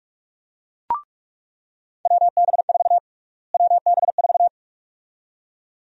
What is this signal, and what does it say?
Morse code, heard as a keyed tone, about 45 wpm, 700 Hz. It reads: WB4 WB4
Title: USA Call Sign Prefixes - Repeated 2x - 45wpm